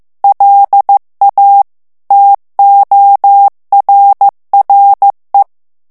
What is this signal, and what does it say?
Morse code, heard as a keyed tone, about 15 wpm, 800 Hz. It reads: LA TORRE
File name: ITALIEN texte clair 1 vitesse 14 mots minute